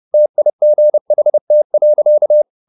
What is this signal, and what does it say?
Morse code, heard as a keyed tone, about 30 wpm, 600 Hz. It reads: TIGHT.
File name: the old man and the sea / the old man and the sea 1 30wpm